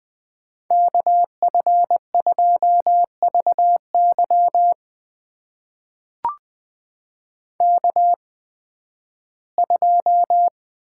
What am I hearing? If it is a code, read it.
Morse code, heard as a keyed tone, about 20 wpm, spaced as for 13 wpm, 700 Hz. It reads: KF2VY K2
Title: USA Call Signs - 8x Character Spacing - 20wpm